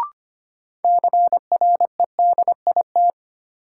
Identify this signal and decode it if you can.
Morse code, heard as a keyed tone, about 25 wpm, 700 Hz. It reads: CREDIT
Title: Sentences from Top 1000 Words - New Word Drill - 25wpm